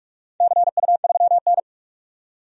Morse code, heard as a keyed tone, about 44 wpm, 700 Hz. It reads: CU3N
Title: DXCC-100 Call Signs - 4 Characters - 45wpm